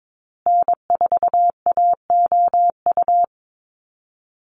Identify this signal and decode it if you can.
Morse code, heard as a keyed tone, about 22 wpm, 700 Hz. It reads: N4AOU
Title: USA Call Signs Repeated 2x 22wpm